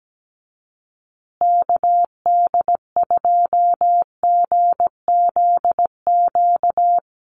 Morse code, heard as a keyed tone, about 17 wpm, 700 Hz. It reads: KD2GZQ
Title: USA Call Signs Repeated 3x 17wpm